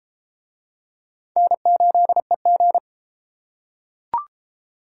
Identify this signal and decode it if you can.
Morse code, heard as a keyed tone, about 31 wpm, 700 Hz. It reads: N8EG